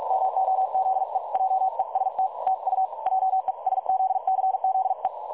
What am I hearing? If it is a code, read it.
Morse code, heard as a keyed tone, about 46 wpm, 760 Hz. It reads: MOZE JEST TU JESZCZE
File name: dlazaawansowanych / nagraniazpasma / websdr recording 3565.0kHz